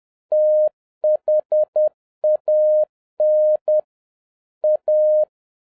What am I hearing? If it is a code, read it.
Morse code, heard as a keyed tone, about 10 wpm, 610 Hz. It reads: THAN A
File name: 2011-07-mp3 / n41 EXCITING QRP CW OPERATION 10wpm